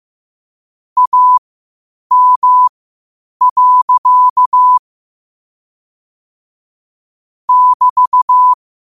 Morse code, heard as a keyed tone, about 15 wpm, spaced as for 8 wpm, 1000 Hz. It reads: AM. <BT>
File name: qotdfiles / 8wpm